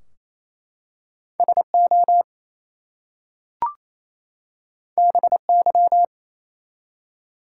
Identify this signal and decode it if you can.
Morse code, heard as a keyed tone, about 29 wpm, 700 Hz. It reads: SO BY
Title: Two Letter Words - 28wpm